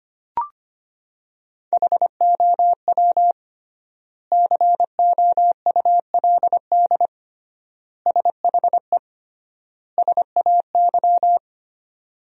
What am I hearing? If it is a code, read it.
Morse code, heard as a keyed tone, about 25 wpm, 700 Hz. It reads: HOW COULD SHE SAY
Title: 3x Word-Spacing - Sentences from Top 200 Words - 25wpm